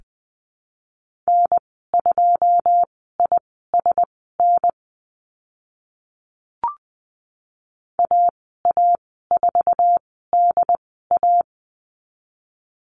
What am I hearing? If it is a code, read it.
Morse code, heard as a keyed tone, about 20 wpm, spaced as for 14 wpm, 700 Hz. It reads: N2ISN AA4DA
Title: 082 Lesson 14g - Number 2 - Call Signs 20wpm